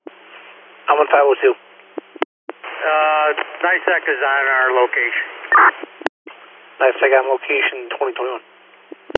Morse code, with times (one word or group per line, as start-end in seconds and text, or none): none